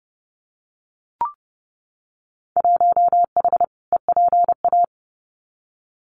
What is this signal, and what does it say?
Morse code, heard as a keyed tone, about 30 wpm, 700 Hz. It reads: 1H EPA
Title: Winter Field Day Exchanges 30wpm